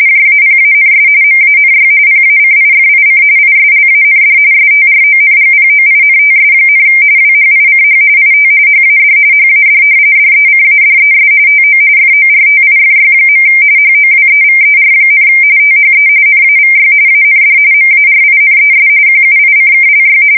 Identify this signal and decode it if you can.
RTTY, about 45.45 baud. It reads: JAPANS MANY PACIFIC ISLAND STRONGPOINTS, WHICH IT HAD HAD DECADES TO FORTIFY. HOWEVER, THE NAVY AND MARINES QUICKLY
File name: fsk170 / battleshipnj